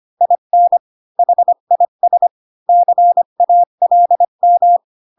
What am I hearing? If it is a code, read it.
Morse code, heard as a keyed tone, about 25 wpm, 700 Hz. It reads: IN HIS CALM
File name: Pinocchio / Pinocchio ch4 20 25 ik1hge